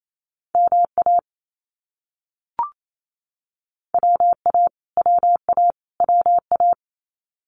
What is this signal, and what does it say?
Morse code, heard as a keyed tone, about 28 wpm, 700 Hz. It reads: MA WA WA WA
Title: US States Repeated 3x 28wpm